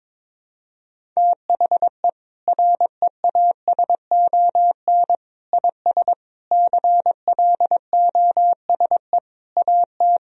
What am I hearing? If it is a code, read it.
Morse code, heard as a keyed tone, about 22 wpm, 700 Hz. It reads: THE REASON IS CLOSE AT
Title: Sentences from Top 1000 Words - Review Part 1 - 22wpm